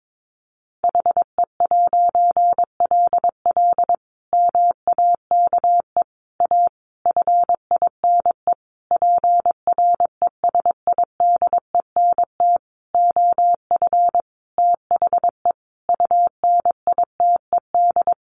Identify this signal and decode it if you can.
Morse code, heard as a keyed tone, about 22 wpm, 700 Hz. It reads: HE'LL MAKE A FINE PRESIDENT OF THE UNITED
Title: A Sound Of Thunder 22wpm